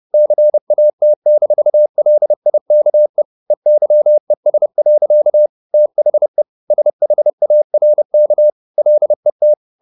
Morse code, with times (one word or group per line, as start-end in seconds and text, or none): 0.1-3.2 CAT-LIKE
3.5-5.5 EYES.
5.7-6.4 THE
6.7-8.5 SHARK
8.8-9.5 LET